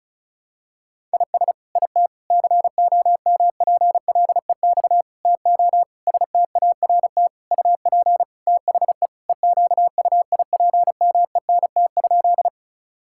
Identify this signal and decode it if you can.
Morse code, heard as a keyed tone, about 36 wpm, 700 Hz. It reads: IS IT COMPLEX TO START UP THE EQUIPMENT?
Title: Sentences from Top 1000 Words - Review Part 6 - 35wpm